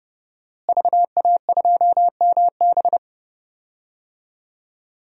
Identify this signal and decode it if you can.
Morse code, heard as a keyed tone, about 30 wpm, 700 Hz. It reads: VA2MB
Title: DXCC-100 Call Signs - 5 Characters - Repeated 3x - 30wpm